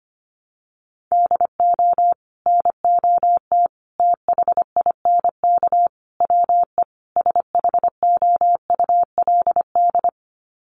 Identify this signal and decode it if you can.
Morse code, heard as a keyed tone, about 25 wpm, 700 Hz. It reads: DO NOT THINK WE SHOULD
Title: Sets of 5 Words Encore 25wpm